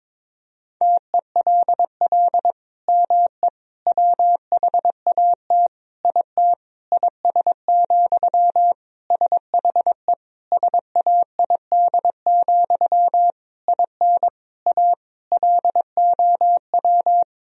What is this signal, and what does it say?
Morse code, heard as a keyed tone, about 22 wpm, 700 Hz. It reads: TELL ME WHAT IT IS, SHE SAID, IN A LOW
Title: Sentences from Top 1000 Words - Review Part 1 - 22wpm